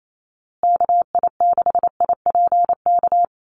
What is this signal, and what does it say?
Morse code, heard as a keyed tone, about 28 wpm, 700 Hz. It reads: KI6IPK
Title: USA Call Signs Repeated 2x 28wpm